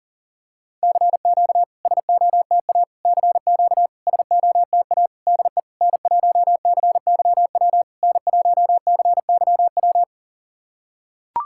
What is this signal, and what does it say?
Morse code, heard as a keyed tone, about 42 wpm, 690 Hz. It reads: CQ SOTA CQ SOTA DE N1CYW N1CYW
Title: SOTA QSO Practice 40wpm